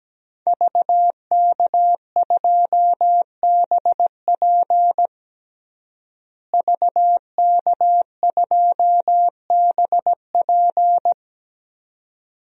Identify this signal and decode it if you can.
Morse code, heard as a keyed tone, about 17 wpm, 700 Hz. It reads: VK2BP VK2BP
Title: DXCC-100 Call Signs - 5 Characters - Repeated 2x - 17wpm